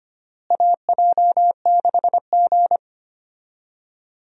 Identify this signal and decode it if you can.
Morse code, heard as a keyed tone, about 25 wpm, 700 Hz. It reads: AJ6G
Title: USA Call Signs - 4 Characters - Repeated 2x - 25wpm